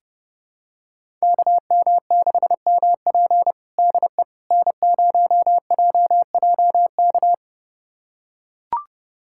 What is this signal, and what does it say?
Morse code, heard as a keyed tone, about 32 wpm, 700 Hz. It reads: KM6MP DE N0JJK
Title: USA Callsign DE Callsign - 5 Characters - 30wpm